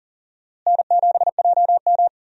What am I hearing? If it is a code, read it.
Morse code, heard as a keyed tone, about 40 wpm, 700 Hz. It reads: N7JM
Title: USA Call Signs - 4 Characters - Repeated 3x - 40wpm